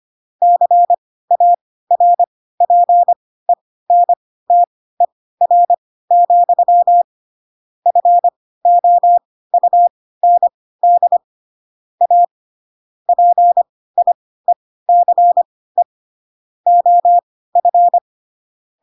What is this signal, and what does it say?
Morse code, heard as a keyed tone, about 25 wpm, spaced as for 16 wpm, 700 Hz. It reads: CARPENTER, FOUND A PIECE OF
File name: Pinocchio / Pinocchio ch1 10 25 ik1hge